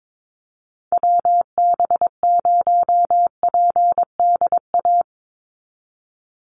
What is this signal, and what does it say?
Morse code, heard as a keyed tone, about 22 wpm, 700 Hz. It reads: WB0PDA
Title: USA Call Signs 22wpm